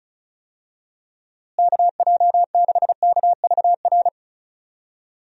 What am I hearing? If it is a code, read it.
Morse code, heard as a keyed tone, about 36 wpm, 700 Hz. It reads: KJ6KVR
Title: USA Call Signs - 6 Characters - Repeated 2x - 35wpm